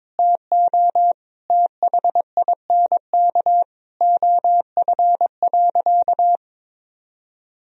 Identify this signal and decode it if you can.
Morse code, heard as a keyed tone, about 22 wpm, 700 Hz. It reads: TO THINK OF.
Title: Sentences from Top 1000 Words - Review Part 1 - 22wpm